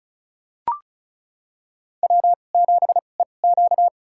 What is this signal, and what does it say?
Morse code, heard as a keyed tone, about 36 wpm, spaced as for 25 wpm, 700 Hz. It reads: W7EQ
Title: USA Call Signs - 4 Characters - 2x Character Spacing - 35wpm